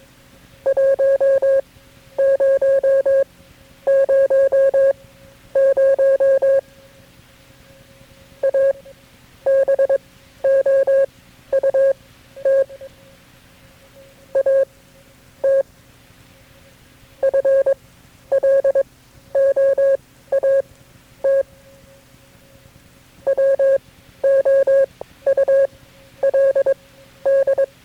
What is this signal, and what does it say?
Morse code, heard as a keyed tone, about 22 wpm, spaced as for 11 wpm, 540 Hz. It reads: 1000 ABOUT AT FLOAT WOULD
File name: KB6MT-tape3-side1